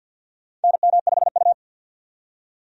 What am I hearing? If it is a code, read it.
Morse code, heard as a keyed tone, about 51 wpm, 700 Hz. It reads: NM5U
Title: USA Call Signs - 4 Characters - 50wpm